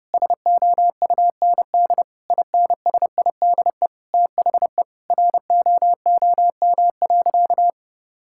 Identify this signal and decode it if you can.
Morse code, heard as a keyed tone, about 30 wpm, 700 Hz. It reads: SOUND INSIDE THE ROOM.
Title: Sentences from Top 1000 Words - Review Part 6 - 30wpm